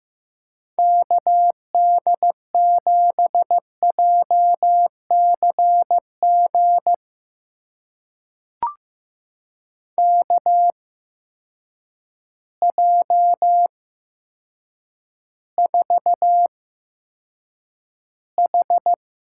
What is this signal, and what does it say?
Morse code, heard as a keyed tone, about 15 wpm, spaced as for 8 wpm, 700 Hz. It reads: KD7JCG KJ4H
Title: USA Call Signs - 6 Characters - 8x Character Spacing - 15wpm